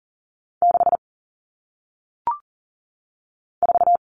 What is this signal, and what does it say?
Morse code, heard as a keyed tone, about 39 wpm, 700 Hz. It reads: 6 4
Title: Numbers - 1-Digit 40wpm